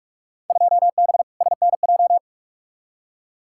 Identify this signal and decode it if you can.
Morse code, heard as a keyed tone, about 46 wpm, 700 Hz. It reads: 2B SNJ
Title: Field Day Exchanges Encore - 45wpm